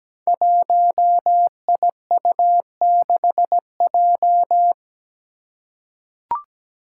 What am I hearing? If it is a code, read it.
Morse code, heard as a keyed tone, about 17 wpm, 700 Hz. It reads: 1IU6J
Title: Five Challenging Letter-Number Combinations 17wpm